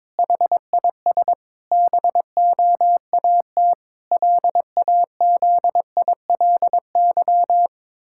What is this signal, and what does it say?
Morse code, heard as a keyed tone, about 22 wpm, 700 Hz. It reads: HIS BOAT LAZILY
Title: Pangram Challenge v2 - 22wpm